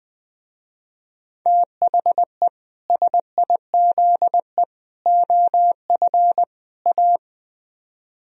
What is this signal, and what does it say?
Morse code, heard as a keyed tone, about 20 wpm, 700 Hz. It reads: THE SIZE OF A
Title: Speed-Racing - Sets of 4 Words 17wpm